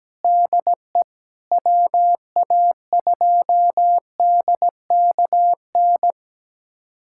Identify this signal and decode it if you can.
Morse code, heard as a keyed tone, about 17 wpm, 700 Hz. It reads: DE WA2DKN
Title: USA Callsign DE Callsign 17wpm